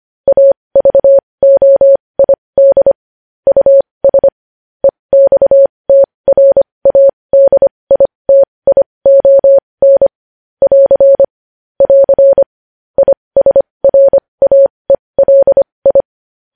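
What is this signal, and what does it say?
Morse code, heard as a keyed tone, about 25 wpm, spaced as for 20 wpm, 550 Hz. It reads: AVOID US EXTRADITION <AR> <AR> ISRAELI
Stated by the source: Morse news headlines practice at 20 wpm.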